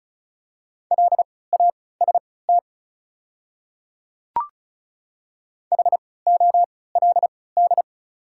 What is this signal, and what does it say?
Morse code, heard as a keyed tone, about 36 wpm, spaced as for 20 wpm, 690 Hz. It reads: LAST HOLD
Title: 3x Character Spacing - Top 500 Words - 35wpm